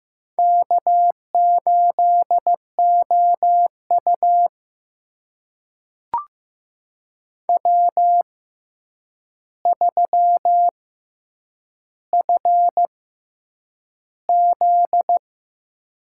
Morse code, heard as a keyed tone, about 15 wpm, spaced as for 8 wpm, 700 Hz. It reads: K8OU W3FZ
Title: USA Call Signs - 4 Characters - 6x Character Spacing - 15wpm